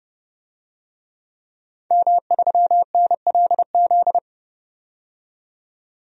Morse code, heard as a keyed tone, about 30 wpm, 700 Hz. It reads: M3NLZ
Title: DXCC-100 Call Signs - 5 Characters - 30wpm